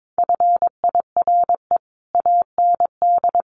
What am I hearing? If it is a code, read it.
Morse code, heard as a keyed tone, about 22 wpm, 700 Hz. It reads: FIRE AND
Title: Sentences from Top 400 Words 22wpm